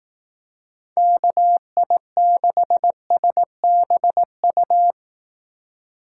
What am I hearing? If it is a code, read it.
Morse code, heard as a keyed tone, about 18 wpm, 700 Hz. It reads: KI6SBU
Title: Speed Racing - USA 2x3 Call Signs 15wpm